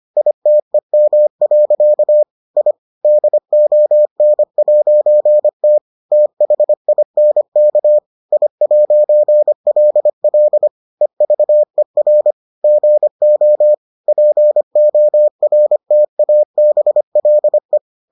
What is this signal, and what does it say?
Morse code, heard as a keyed tone, about 25 wpm, 600 Hz. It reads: ITEM. I DON'T THINK I'LL EVER GO PORTABLE